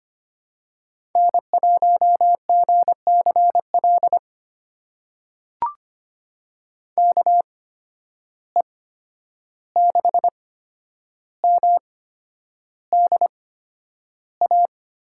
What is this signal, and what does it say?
Morse code, heard as a keyed tone, about 25 wpm, spaced as for 10 wpm, 700 Hz. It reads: N1GCL KE6MDA
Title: USA Call Signs - 8x Character Spacing - 25wpm